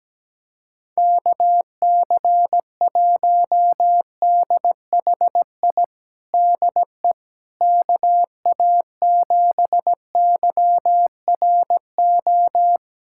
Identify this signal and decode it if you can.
Morse code, heard as a keyed tone, about 17 wpm, 700 Hz. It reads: KC1DHI DE KA7YRO
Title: USA Callsign DE Callsign - 6 Characters - 17wpm